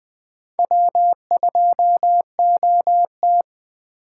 Morse code, heard as a keyed tone, about 20 wpm, 700 Hz. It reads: W2OT